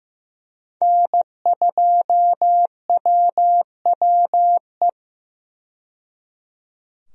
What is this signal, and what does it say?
Morse code, heard as a keyed tone, about 15 wpm, 700 Hz. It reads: N2WWE
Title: USA Call Signs - 5 Characters - 15wpm